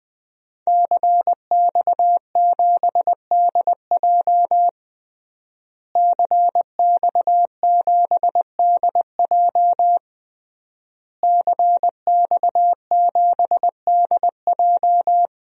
Morse code, heard as a keyed tone, about 20 wpm, 700 Hz. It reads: CX7DJ CX7DJ CX7DJ
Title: DXCC-100 Call Signs - 5 Characters - Repeated 3x - 20wpm